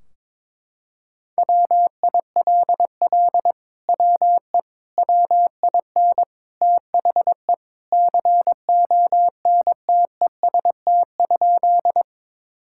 Morse code, heard as a keyed tone, about 22 wpm, 700 Hz. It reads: WILL WE WIN THE CONTEST?
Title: Sentences from Top 1600 Words 22wpm